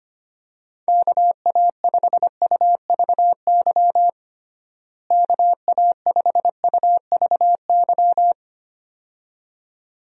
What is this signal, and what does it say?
Morse code, heard as a keyed tone, about 25 wpm, 700 Hz. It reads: KA5UVY KA5UVY
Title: USA Call Signs - 6 Characters - Repeated 2x - 25wpm